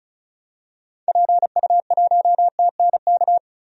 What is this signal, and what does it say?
Morse code, heard as a keyed tone, about 36 wpm, 700 Hz. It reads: PU1TNK
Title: DXCC-100 Call Signs - 6 Characters - Repeated 3x - 35wpm